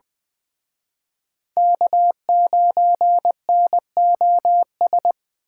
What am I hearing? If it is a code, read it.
Morse code, heard as a keyed tone, about 24 wpm, 700 Hz. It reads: K9NOS
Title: Speed Racing - USA 1x3 Call Signs 20wpm